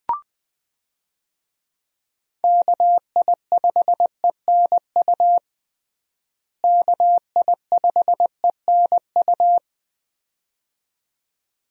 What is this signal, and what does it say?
Morse code, heard as a keyed tone, about 20 wpm, 700 Hz. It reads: KI5ENU KI5ENU
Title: USA Call Signs - 6 Characters - Repeated 2x - 20wpm